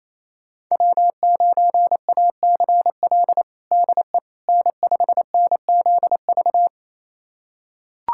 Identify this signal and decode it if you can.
Morse code, heard as a keyed tone, about 27 wpm, 700 Hz. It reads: W9ACL DE N5NZV
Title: USA Callsign DE Callsign - 5 Characters - 28wpm